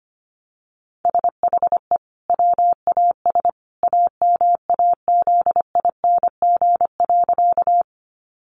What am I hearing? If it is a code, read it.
Morse code, heard as a keyed tone, about 25 wpm, 700 Hz. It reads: SHE WAS AMAZING.
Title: Sentences from Top 1500 Words 25wpm